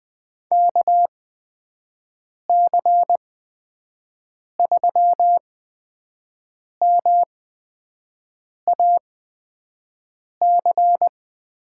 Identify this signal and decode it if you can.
Morse code, heard as a keyed tone, about 20 wpm, spaced as for 5 wpm, 700 Hz. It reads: KC3MAC
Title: USA Call Signs - 8x Character Spacing - 20wpm